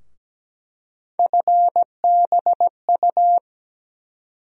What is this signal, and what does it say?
Morse code, heard as a keyed tone, about 17 wpm, 700 Hz. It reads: FBU
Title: USA Call Sign Postfixes - 3 Letters 17wpm